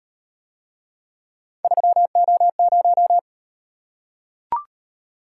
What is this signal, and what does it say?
Morse code, heard as a keyed tone, about 38 wpm, 700 Hz. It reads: 3O0